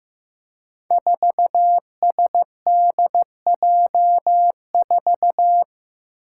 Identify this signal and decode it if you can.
Morse code, heard as a keyed tone, about 15 wpm, 700 Hz. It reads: 4SDJ4
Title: Five Challenging Letter-Number Combinations 15wpm